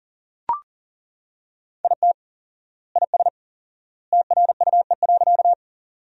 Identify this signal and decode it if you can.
Morse code, heard as a keyed tone, about 39 wpm, 700 Hz. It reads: IT IS TRUE.
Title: 4x Word-Spacing - Sentences from Top 400 Words - 40wpm